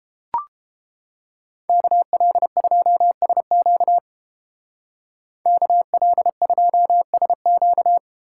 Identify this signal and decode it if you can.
Morse code, heard as a keyed tone, about 31 wpm, 700 Hz. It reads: KL2SQ KL2SQ